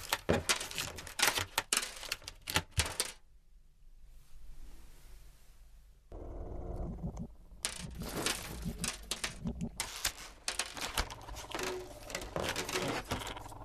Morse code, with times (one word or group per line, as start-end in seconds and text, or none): none